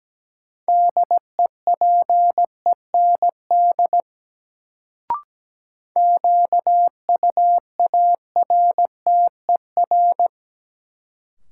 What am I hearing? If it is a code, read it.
Morse code, heard as a keyed tone, about 17 wpm, 700 Hz. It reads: DEPEND QUARTER
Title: Sentences from Top 1300 Words - New Word Drill Alt - 17wpm